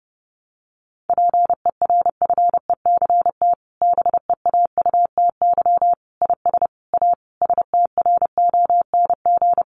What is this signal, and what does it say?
Morse code, heard as a keyed tone, about 30 wpm, 700 Hz. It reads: PERFECT BEAUTY IS A STRONG
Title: Sentences from Top 1200 Words 30wpm